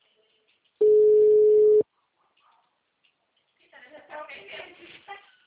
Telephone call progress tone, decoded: ringback tone

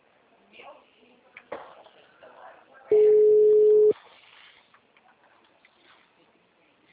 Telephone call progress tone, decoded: ringback tone

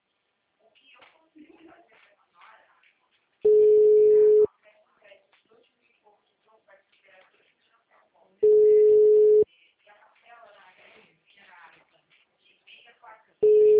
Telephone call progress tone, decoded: ringback tone

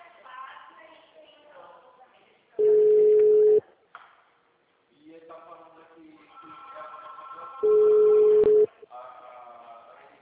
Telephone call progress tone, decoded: ringback tone

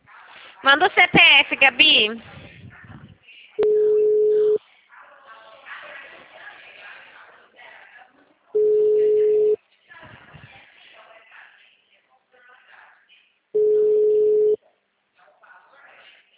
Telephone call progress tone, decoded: ringback tone

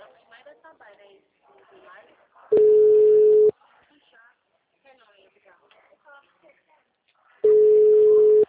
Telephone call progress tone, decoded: ringback tone

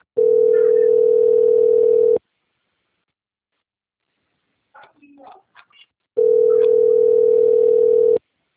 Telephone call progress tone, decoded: ringback tone

